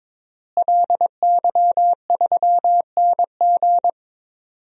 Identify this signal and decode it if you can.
Morse code, heard as a keyed tone, about 21 wpm, 700 Hz. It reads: LY3NG
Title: DXCC-100 Call Signs - 5 Characters - Repeated 2x - 22wpm